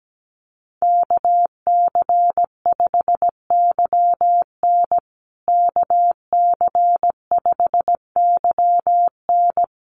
Morse code, heard as a keyed tone, about 17 wpm, 700 Hz. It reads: KC5YN KC5YN
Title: USA Call Signs Repeated 2x 17wpm